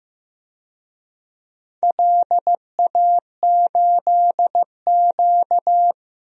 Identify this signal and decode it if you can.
Morse code, heard as a keyed tone, about 15 wpm, 700 Hz. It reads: LA8Q
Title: DXCC-100 Call Signs - 4 Characters - 15wpm